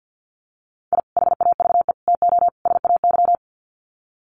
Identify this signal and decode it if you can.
Morse code, heard as a keyed tone, about 50 wpm, 700 Hz. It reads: I HAVE TO SAY
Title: Sets of 4 Words 50wpm